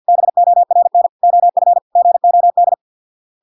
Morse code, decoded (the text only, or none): BORN OF GOD